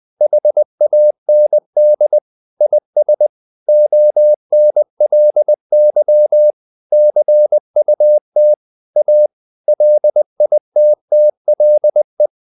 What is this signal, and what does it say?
Morse code, heard as a keyed tone, about 20 wpm, 600 Hz. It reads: HAND IS ONLY CUT A LITTLE
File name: the old man and the sea part 2 20wpm